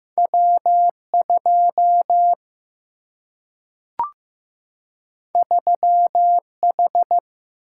Morse code, heard as a keyed tone, about 15 wpm, 700 Hz. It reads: W 2 3 H
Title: Single-Letter Single-Number 15wpm